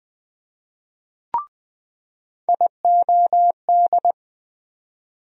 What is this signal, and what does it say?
Morse code, heard as a keyed tone, about 20 wpm, 700 Hz. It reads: IOD
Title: USA Call Sign Postfixes - 3 Letters 20wpm